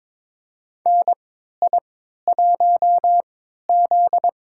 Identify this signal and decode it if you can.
Morse code, heard as a keyed tone, about 22 wpm, spaced as for 13 wpm, 700 Hz. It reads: NI1Z
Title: USA Call Signs - 4 Characters - 3x Character Spacing - 22wpm